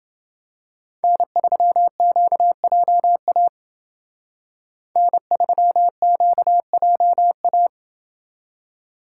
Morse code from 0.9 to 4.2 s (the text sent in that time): N3QJA